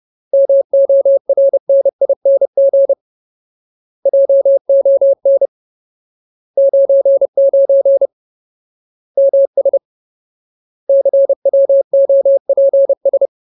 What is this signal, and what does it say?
Morse code, heard as a keyed tone, about 30 wpm, 550 Hz. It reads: MORNING JON 99 MS CWOPS
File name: Ultimate Word List / 30WPM Ultimate